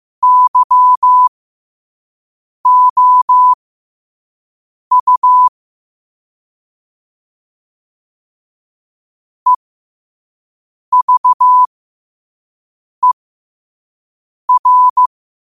Morse code, heard as a keyed tone, about 15 wpm, spaced as for 5 wpm, 1000 Hz. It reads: YOU EVER
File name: 5wpm